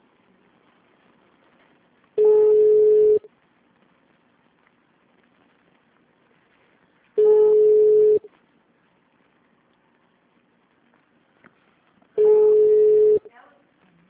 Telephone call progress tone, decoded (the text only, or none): ringback tone